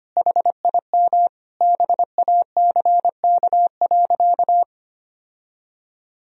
Morse code, heard as a keyed tone, about 25 wpm, 700 Hz. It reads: HIM BACK.
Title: Sentences from Top 1000 Words - Review Part 3 - 25wpm